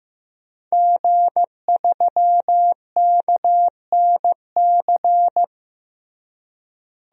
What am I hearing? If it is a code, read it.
Morse code, heard as a keyed tone, about 15 wpm, 700 Hz. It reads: G3KNC
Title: DXCC-100 Call Signs - 5 Characters - Repeated 3x - 15wpm